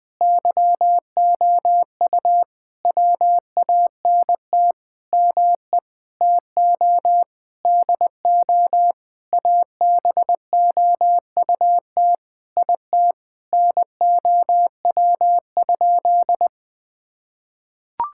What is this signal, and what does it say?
Morse code, heard as a keyed tone, about 21 wpm, 700 Hz. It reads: YOU WANT ME TO DO ABOUT IT NOW?
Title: Sentences with 7-Word Phrases - 20wpm